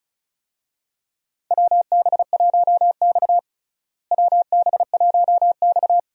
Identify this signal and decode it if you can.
Morse code, heard as a keyed tone, about 36 wpm, 700 Hz. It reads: WB1X WB1X
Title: USA Call Signs - 4 Characters - Repeated 2x - 35wpm